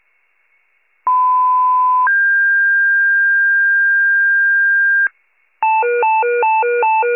Two-tone page, heard as a one-second tone, then a three-second tone, about 1010 then 1600 Hz. It is heard from 1.1 to 5.1 s.